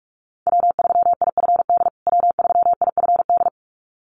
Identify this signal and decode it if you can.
Morse code, heard as a keyed tone, about 44 wpm, 700 Hz. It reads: W3IFD W3IFD
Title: USA Call Signs Repeated 2x 45wpm